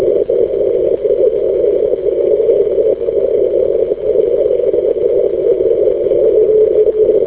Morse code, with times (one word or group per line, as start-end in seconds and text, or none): none